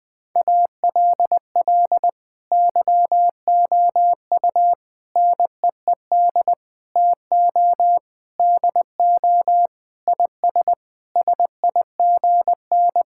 Morse code, heard as a keyed tone, about 21 wpm, 700 Hz. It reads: ALL YOU NEED TO DO IS SIGN
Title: Sentences with 6-Word Phrases - 20wpm